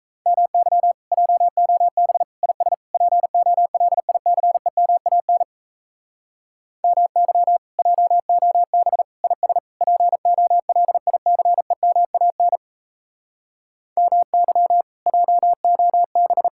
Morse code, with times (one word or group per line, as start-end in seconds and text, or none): 0.3-0.9 MY
1.1-2.2 JOB
2.4-2.7 IS
2.9-5.4 POLICEMAN
6.8-7.6 MY
7.8-9.0 JOB
9.2-9.6 IS
9.8-12.6 POLICEMAN
14.0-14.8 MY
15.1-16.5 JOB